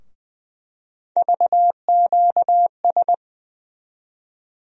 Morse code, heard as a keyed tone, about 21 wpm, 700 Hz. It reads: VQS
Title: USA Call Sign Postfixes - 3 Letters 20wpm